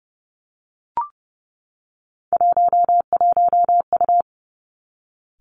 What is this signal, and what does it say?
Morse code, heard as a keyed tone, about 30 wpm, 700 Hz. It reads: 11U
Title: Three Challenging Letter-Number Combinations 30wpm